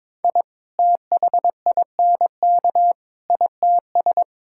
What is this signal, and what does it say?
Morse code, heard as a keyed tone, about 22 wpm, 700 Hz. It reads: I THINK ITS
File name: practice / 22wpm / Sets of 6 Words - 22wpm